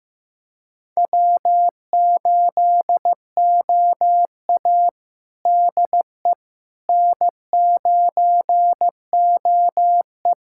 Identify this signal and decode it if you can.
Morse code, heard as a keyed tone, about 15 wpm, 700 Hz. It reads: W8OA DE N9OE
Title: USA Callsign DE Callsign - 4 Characters - 15wpm